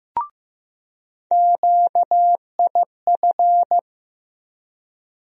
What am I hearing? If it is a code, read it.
Morse code, heard as a keyed tone, about 15 wpm, 700 Hz. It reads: QIF
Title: USA Call Sign Postfixes - 3 Letters 15wpm